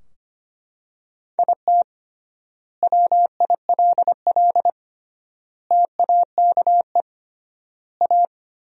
Morse code, heard as a keyed tone, about 25 wpm, 700 Hz. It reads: IT WILL TAKE A
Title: 3x Word-Spacing - Sentences from Top 200 Words - 25wpm